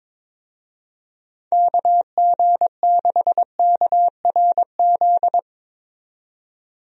Morse code, heard as a keyed tone, about 22 wpm, 700 Hz. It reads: KG6KRZ
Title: USA Call Signs - 6 Characters - 22wpm